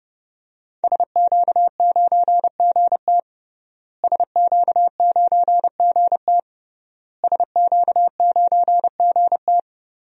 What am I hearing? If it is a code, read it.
Morse code, heard as a keyed tone, about 30 wpm, 700 Hz. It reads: SQ9GT SQ9GT SQ9GT
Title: DXCC-100 Call Signs - 5 Characters - Repeated 3x - 30wpm